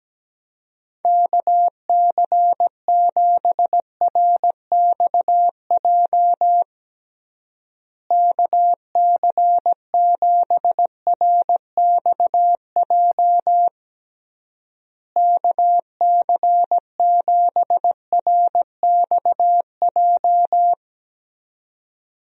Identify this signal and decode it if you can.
Morse code, heard as a keyed tone, about 17 wpm, 700 Hz. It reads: KC7RXJ KC7RXJ KC7RXJ
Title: USA Call Signs - 6 Characters - Repeated 3x - 17wpm